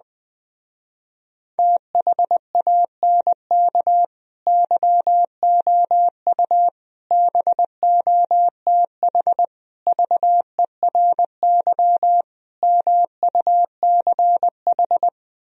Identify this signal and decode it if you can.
Morse code, heard as a keyed tone, about 20 wpm, 700 Hz. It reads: THANK YOU BOTH VERY MUCH